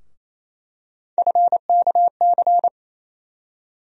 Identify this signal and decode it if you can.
Morse code, heard as a keyed tone, about 28 wpm, 700 Hz. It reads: FKC
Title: USA Call Sign Postfixes - 3 Letters 28wpm